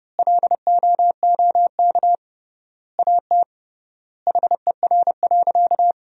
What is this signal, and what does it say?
Morse code, heard as a keyed tone, about 32 wpm, 700 Hz. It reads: LOOK AT HER.
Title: 3x Word-Spacing - Sentences from Top 200 Words - 30wpm